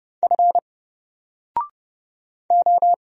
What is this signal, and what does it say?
Morse code, heard as a keyed tone, about 29 wpm, 700 Hz. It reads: F O
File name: Single-Letter-Number-30wpm